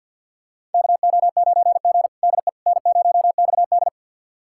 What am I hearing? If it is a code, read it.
Morse code, heard as a keyed tone, about 47 wpm, 690 Hz. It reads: KO9G DE N0XD
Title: USA Callsign DE Callsign - 4 Characters - 50wpm